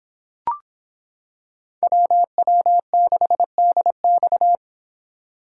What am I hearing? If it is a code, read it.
Morse code, heard as a keyed tone, about 26 wpm, 700 Hz. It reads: WW6DX